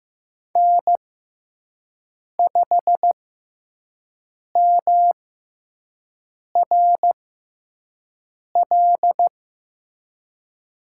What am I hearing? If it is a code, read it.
Morse code, heard as a keyed tone, about 15 wpm, spaced as for 5 wpm, 700 Hz. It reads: N5MRL
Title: USA Call Signs - 6x Character Spacing - 15wpm